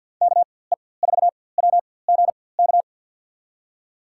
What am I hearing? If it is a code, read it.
Morse code, heard as a keyed tone, about 48 wpm, spaced as for 23 wpm, 700 Hz. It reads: KE4WGK
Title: USA Call Signs - 6 Characters - 4x Character Spacing - 50wpm